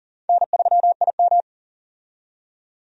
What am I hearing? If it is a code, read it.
Morse code, heard as a keyed tone, about 38 wpm, 700 Hz. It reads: N3IM
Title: USA Call Signs - 4 Characters - 40wpm